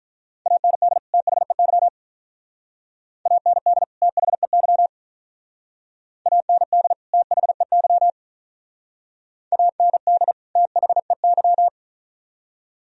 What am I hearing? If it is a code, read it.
Morse code, heard as a keyed tone, about 45 wpm, 700 Hz. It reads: AND THEY AND THEY AND THEY AND THEY